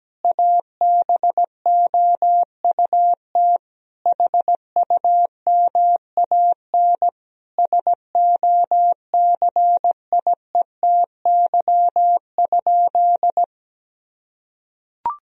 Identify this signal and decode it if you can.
Morse code, heard as a keyed tone, about 17 wpm, 700 Hz. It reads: ABOUT HUMAN SOCIETY?
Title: Sentences from Top 1000 Words - Review Part 4 - 17wpm